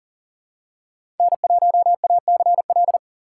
Morse code, heard as a keyed tone, about 40 wpm, 700 Hz. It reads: N1ACL
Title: USA Call Signs - 5 Characters - 6x Character Spacing - 40wpm